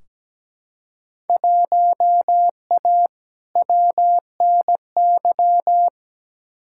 Morse code, heard as a keyed tone, about 17 wpm, 700 Hz. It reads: 1A WNY
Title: Field Day Exchanges - 17wpm